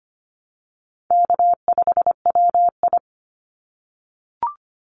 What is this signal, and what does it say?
Morse code, heard as a keyed tone, about 25 wpm, 700 Hz. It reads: K5WI
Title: NA SOTA Chaser Callsigns 25wpm